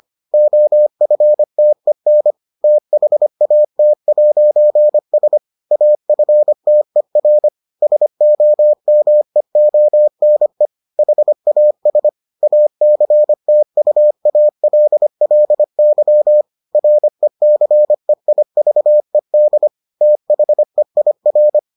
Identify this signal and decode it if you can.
Morse code, heard as a keyed tone, about 25 wpm, 600 Hz. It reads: OFTEN THAT'S AFTER SOMEONE HAS ACTUALLY RECEIVED THEIR